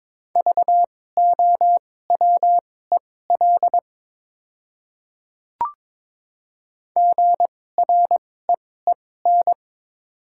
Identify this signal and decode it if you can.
Morse code, heard as a keyed tone, about 21 wpm, spaced as for 16 wpm, 700 Hz. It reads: VOWEL GREEN
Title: 2x Character Spacing - Top 500 Words - 22wpm